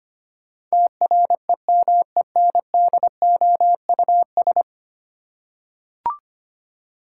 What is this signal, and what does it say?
Morse code, heard as a keyed tone, about 25 wpm, 700 Hz. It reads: TREMENDOUS
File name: 10 Letter Words - First 3 Letters Given - 25wpm